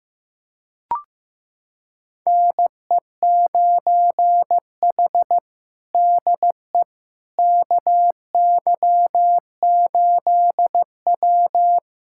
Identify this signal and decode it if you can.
Morse code, heard as a keyed tone, about 16 wpm, 700 Hz. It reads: NE9H DE KY8W
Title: USA Callsign DE Callsign - 4 Characters - 15wpm